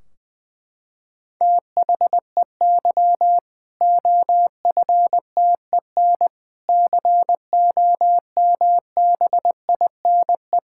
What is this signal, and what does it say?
Morse code, heard as a keyed tone, about 20 wpm, 700 Hz. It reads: THEY OFTEN COMBINE
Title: Sentences from Top 1600 Words 20wpm